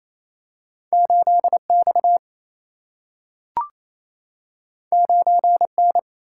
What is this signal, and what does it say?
Morse code, heard as a keyed tone, about 28 wpm, 700 Hz. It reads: 8 X 9 N
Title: Single-Letter Single-Number 28wpm